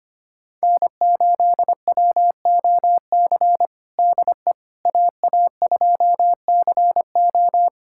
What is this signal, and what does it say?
Morse code, heard as a keyed tone, about 25 wpm, 700 Hz. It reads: N8WOC DE AA2CO
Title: USA Callsign DE Callsign - 5 Characters - 25wpm